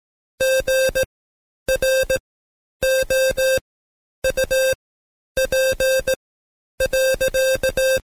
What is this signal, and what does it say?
Morse code, heard as a keyed tone, about 17 wpm, spaced as for 10 wpm, 530 Hz. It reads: GROUP.
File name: Code Quick / AudioDownld2 / Seg131-Stepper4 9WPM